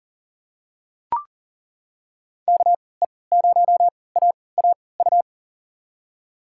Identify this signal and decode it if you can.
Morse code, heard as a keyed tone, about 40 wpm, spaced as for 23 wpm, 700 Hz. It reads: KE0AAU
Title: USA Call Signs - 6 Characters - 3x Character Spacing - 40wpm